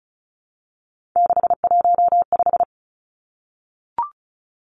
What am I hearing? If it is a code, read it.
Morse code, heard as a keyed tone, about 36 wpm, 700 Hz. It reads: 615
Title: Three Challenging Letter-Number Combinations 35wpm